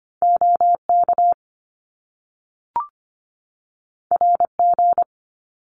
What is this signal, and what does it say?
Morse code, heard as a keyed tone, about 25 wpm, 700 Hz. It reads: OK RG
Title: Top Two Letter Combinations in Words 25wpm